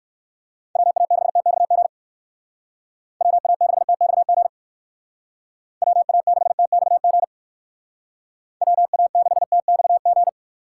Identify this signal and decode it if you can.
Morse code, heard as a keyed tone, about 59 wpm, 700 Hz. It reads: WA6TXG WA6TXG WA6TXG WA6TXG